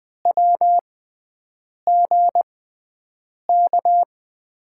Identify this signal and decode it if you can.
Morse code, heard as a keyed tone, about 20 wpm, spaced as for 7 wpm, 700 Hz. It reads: WGK
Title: USA Call Signs - 5 Characters - 6x Character Spacing - 20wpm